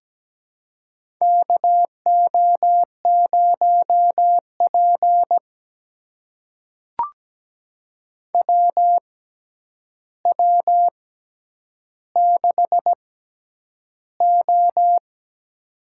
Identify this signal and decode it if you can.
Morse code, heard as a keyed tone, about 17 wpm, spaced as for 9 wpm, 700 Hz. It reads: KO0P WW6O
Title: USA Call Signs - 4 Characters - 6x Character Spacing - 17wpm